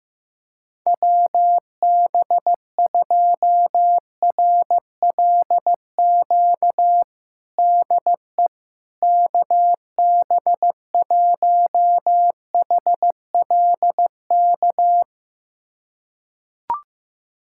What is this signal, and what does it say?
Morse code, heard as a keyed tone, about 15 wpm, 700 Hz. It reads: WB2RLQ DE KB1HLK
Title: USA Callsign DE Callsign - 6 Characters - 15wpm